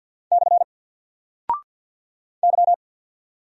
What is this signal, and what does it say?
Morse code, heard as a keyed tone, about 50 wpm, 700 Hz. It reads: / Y
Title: Single Character (Letter:Number:Punctuation:Prosign) 50wpm